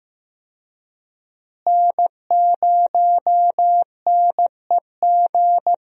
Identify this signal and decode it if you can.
Morse code, heard as a keyed tone, about 15 wpm, 700 Hz. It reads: N0NEG
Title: USA Call Signs - 5 Characters - Repeated 3x - 15wpm